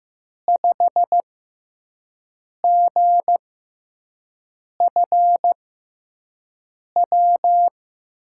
Morse code, heard as a keyed tone, about 15 wpm, spaced as for 5 wpm, 700 Hz. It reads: 5GFW
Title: USA Call Signs - 5 Characters - 6x Character Spacing - 15wpm